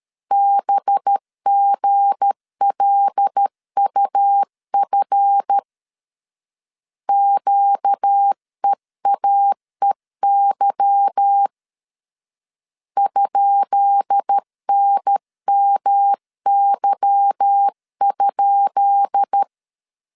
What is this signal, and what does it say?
Morse code, heard as a keyed tone, about 13 wpm, 800 Hz. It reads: BGLUF QEAEY ?NMY?
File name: LEZIONI / 10 SIGNS / 10B INT 12wpm 2414